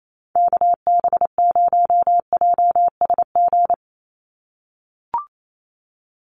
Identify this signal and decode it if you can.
Morse code, heard as a keyed tone, about 28 wpm, 700 Hz. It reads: KB0JSG
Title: USA Call Signs 28wpm